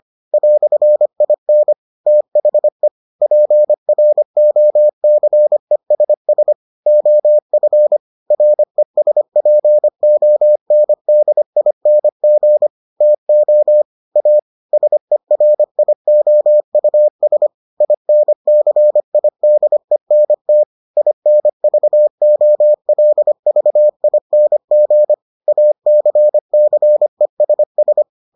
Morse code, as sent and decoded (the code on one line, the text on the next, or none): .-..-. .. -. / - .... / .--. .-. --- -.-. ... ... / --- ..-. / .-. ... .--. --- -. -.. .. -. --. / - --- / .- / ... .-. .. --- ..- ... / .. -. -.-. .. -.. -. - / .. -. ...- --- .-.. ...- .. -. --. / .- -.-. -.-. ... ...
"IN THE PROCESS OF RESPONDING TO A SERIOUS INCIDENT INVOLVING ACCESS